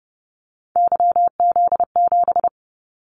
Morse code, heard as a keyed tone, about 30 wpm, 700 Hz. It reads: YZ7
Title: Callsign Prefixes for Top 25 DXCC - Repeated 3x - 30wpm